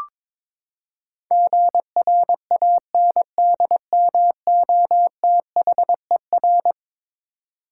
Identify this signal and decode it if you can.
Morse code, heard as a keyed tone, about 21 wpm, 700 Hz. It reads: GRANDMOTHER
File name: practice / 22wpm / Eleven Letter Words - 22wpm